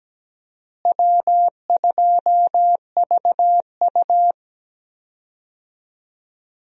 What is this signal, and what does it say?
Morse code, heard as a keyed tone, about 17 wpm, 700 Hz. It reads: W2VU
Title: USA Call Signs - 4 Characters - Repeated 2x - 17wpm